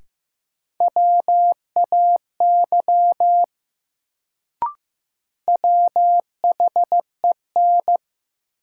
Morse code, heard as a keyed tone, about 15 wpm, 700 Hz. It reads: WAY WHEN
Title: Sentences from Top 100 Words - New Word Drill - 15wpm